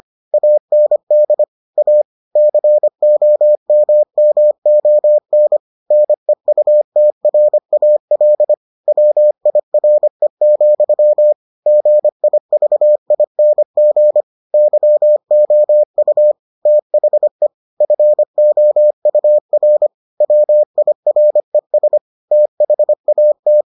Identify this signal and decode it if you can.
Morse code, heard as a keyed tone, about 25 wpm, 600 Hz. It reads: AND A COMMON NEUTRAL WIRE, GIVING YOU THE FOUR WIRES THAT